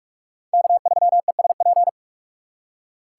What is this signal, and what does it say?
Morse code, heard as a keyed tone, about 40 wpm, 700 Hz. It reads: K3ESP